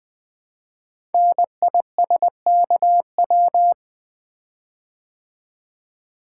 Morse code, heard as a keyed tone, about 20 wpm, 700 Hz. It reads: NISKW
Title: Code Groups - 5 Letters - 20wpm